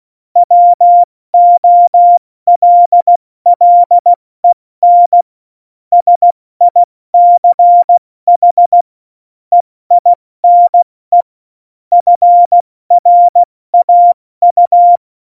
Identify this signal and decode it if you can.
Morse code, heard as a keyed tone, about 16 wpm, 700 Hz. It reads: WOLLEN SICH EINE FRAU
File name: Morsekurs / Witze / Witz 29. Dezember 2020, 70 F80